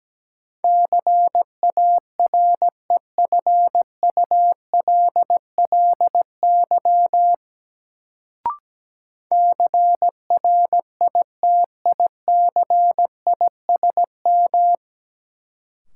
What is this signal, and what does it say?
Morse code, heard as a keyed tone, about 17 wpm, 700 Hz. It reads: CAREFULLY CRITICISM
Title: Sentences from Top 1300 Words - New Word Drill Alt - 17wpm